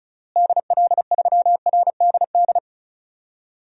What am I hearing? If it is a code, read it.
Morse code, heard as a keyed tone, about 34 wpm, 700 Hz. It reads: DL3RDD
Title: DXCC-100 Call Signs - 6 Characters - Repeated 3x - 35wpm